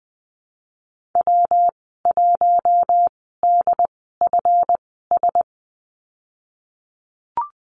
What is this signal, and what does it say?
Morse code, heard as a keyed tone, about 20 wpm, spaced as for 15 wpm, 700 Hz. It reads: W1DFS
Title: 173 Lesson 28g - Number 9 - Call Signs 20wpm